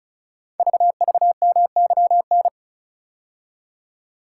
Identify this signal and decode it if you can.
Morse code, heard as a keyed tone, about 36 wpm, 690 Hz. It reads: VVMYN
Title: Code Groups - 5 Letters - 35wpm